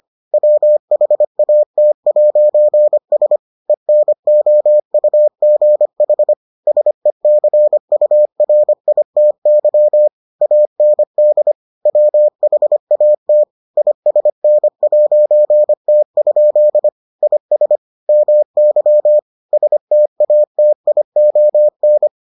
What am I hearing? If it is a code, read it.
Morse code, heard as a keyed tone, about 25 wpm, 600 Hz. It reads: WHAT'S ENOUGH SECURITY AND WHAT ISN'T? IS MY STATION